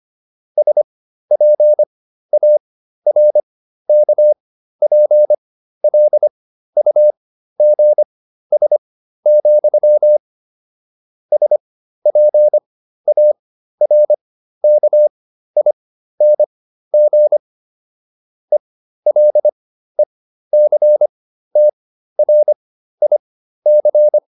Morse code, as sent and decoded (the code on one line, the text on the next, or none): ... .--. .- .-. -.- .--. .-.. ..- --. ... --..-- / ... .--. .- .-. -.- .. -. --. / . .-.. -.-. - .-. .. -.-.
SPARKPLUGS, SPARKING ELECTRIC